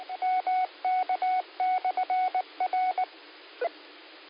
Morse code, heard as a keyed tone, about 19 wpm, 740 Hz, with radio noise. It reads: MK/R